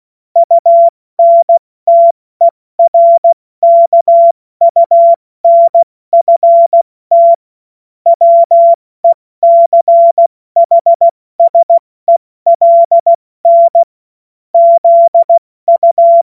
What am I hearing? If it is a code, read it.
Morse code, heard as a keyed tone, about 16 wpm, 680 Hz. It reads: UNTERKUNFT WECHSELN ZU
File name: Morsekurs / Witze / Witz 30. August 2022 70 F80